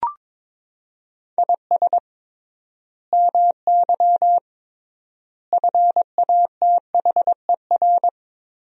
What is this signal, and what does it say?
Morse code, heard as a keyed tone, about 22 wpm, 700 Hz. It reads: IS MY FATHER
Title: 3x Word-Spacing - Sentences from Top 500 Words - 22wpm